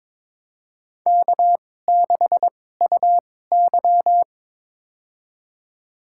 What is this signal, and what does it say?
Morse code, heard as a keyed tone, about 22 wpm, spaced as for 16 wpm, 700 Hz. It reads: K6UY
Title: USA Call Signs - 4 Characters - 2x Character Spacing - 22wpm